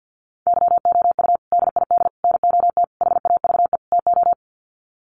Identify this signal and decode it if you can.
Morse code, heard as a keyed tone, about 50 wpm, 700 Hz. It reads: YOU DID NOT HAVE TO
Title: Sets of 5 Words Encore 50wpm